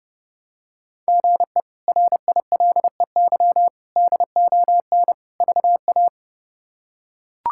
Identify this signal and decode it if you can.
Morse code, heard as a keyed tone, about 31 wpm, 700 Hz. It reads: GE RILEY DON VA
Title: K1USN SST QSO Practice - 30wpm